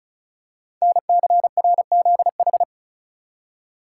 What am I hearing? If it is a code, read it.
Morse code, heard as a keyed tone, about 36 wpm, 700 Hz. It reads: NCRZH
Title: Code Groups - 5 Letters - 35wpm